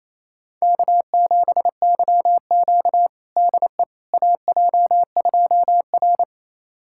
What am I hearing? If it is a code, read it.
Morse code, heard as a keyed tone, about 27 wpm, 700 Hz. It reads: K7YQ DE AJ2R
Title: USA Callsign DE Callsign - 4 Characters - 28wpm